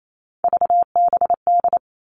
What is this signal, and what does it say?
Morse code, heard as a keyed tone, about 28 wpm, 700 Hz. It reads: VBD
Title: All Cut Numbers - 3-Digit 28wpm